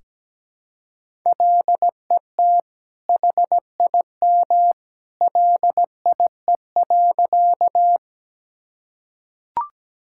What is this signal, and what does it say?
Morse code, heard as a keyed tone, about 17 wpm, 700 Hz. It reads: LET HIM LIE.
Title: Sentences from Top 1600 Words 17wpm